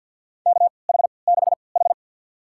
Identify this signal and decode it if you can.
Morse code, heard as a keyed tone, about 51 wpm, spaced as for 28 wpm, 700 Hz. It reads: KH6H
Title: USA Call Signs - 3x Character Spacing - 50wpm